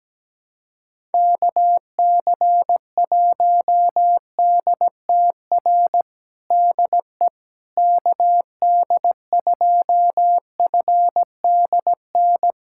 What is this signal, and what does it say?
Morse code, heard as a keyed tone, about 17 wpm, 700 Hz. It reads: KC1DTR DE KD2FDN
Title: USA Callsign DE Callsign 17wpm